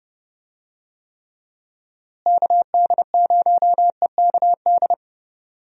Morse code, heard as a keyed tone, about 30 wpm, 700 Hz. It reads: KD0EKD
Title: USA Call Signs - 6 Characters - Repeated 3x - 30wpm